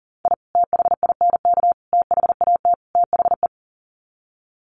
Morse code, heard as a keyed tone, about 40 wpm, 700 Hz. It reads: I THINK THAT THE
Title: Sets of 4 Words 40wpm